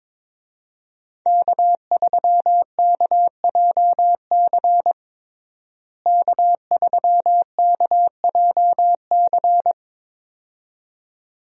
Morse code, heard as a keyed tone, about 22 wpm, 700 Hz. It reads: K3KJC K3KJC
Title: USA Call Signs - 5 Characters - Repeated 2x - 22wpm